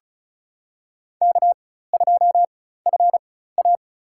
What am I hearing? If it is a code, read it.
Morse code, heard as a keyed tone, about 36 wpm, spaced as for 16 wpm, 700 Hz. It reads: K2FA
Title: USA Call Signs - 4 Characters - 4x Character Spacing - 35wpm